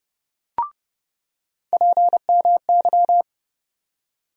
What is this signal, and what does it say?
Morse code, heard as a keyed tone, about 31 wpm, 700 Hz. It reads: PMY
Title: USA Call Sign Postfixes - 3 Letters 30wpm